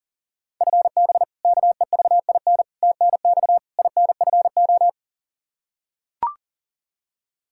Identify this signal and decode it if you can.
Morse code, heard as a keyed tone, about 40 wpm, 700 Hz. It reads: FB KEVIN TNX INFO
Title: Practice QSO - Exchange 3 - More Info - 40wpm